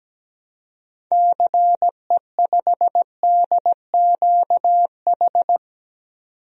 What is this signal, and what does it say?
Morse code, heard as a keyed tone, about 17 wpm, 700 Hz. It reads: CE5DQH
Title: DXCC-100 Call Signs - 6 Characters - 17wpm